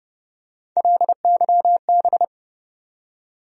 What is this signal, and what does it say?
Morse code, heard as a keyed tone, about 30 wpm, 700 Hz. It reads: LYB
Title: USA Call Sign Postfixes - 3 Letters 30wpm